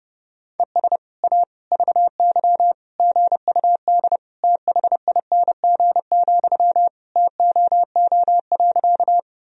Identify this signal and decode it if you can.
Morse code, heard as a keyed tone, about 30 wpm, 700 Hz. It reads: ES A VY GUD THING, TOO.
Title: Sentences from Top 200 Words and Abbreviations 30wpm